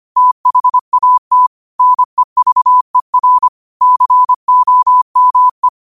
Morse code, heard as a keyed tone, about 25 wpm, 1000 Hz. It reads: THAT NEVER COME
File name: qotdfiles / 25wpm